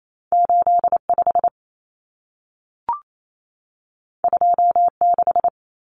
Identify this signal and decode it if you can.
Morse code, heard as a keyed tone, about 28 wpm, 700 Hz. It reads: 85 26
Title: Numbers - 2-Digit 28wpm